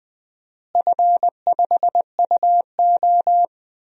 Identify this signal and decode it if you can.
Morse code, heard as a keyed tone, about 20 wpm, 700 Hz. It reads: F5UO
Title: DXCC-100 Call Signs - 4 Characters - Repeated 2x - 20wpm